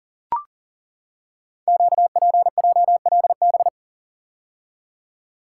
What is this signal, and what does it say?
Morse code, heard as a keyed tone, about 42 wpm, 700 Hz. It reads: QPJLB
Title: Code Groups - 5 Letters - 40wpm